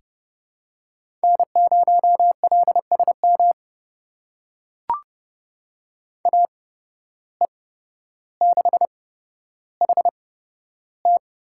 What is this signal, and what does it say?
Morse code, heard as a keyed tone, about 30 wpm, spaced as for 13 wpm, 700 Hz. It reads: N0LSM AE6HT
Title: USA Call Signs - 5 Characters - 8x Character Spacing - 30wpm